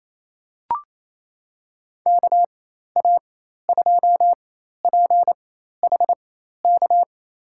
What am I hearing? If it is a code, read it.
Morse code, heard as a keyed tone, about 28 wpm, spaced as for 13 wpm, 700 Hz. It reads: KA2PHK
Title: USA Call Signs - 4x Character Spacing - 28wpm